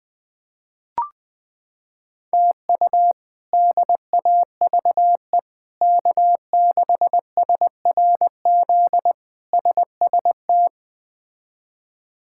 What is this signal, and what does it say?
Morse code, heard as a keyed tone, about 21 wpm, 700 Hz. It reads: TU DAVE K6SRZ SST
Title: K1USN SST QSO Practice - 20wpm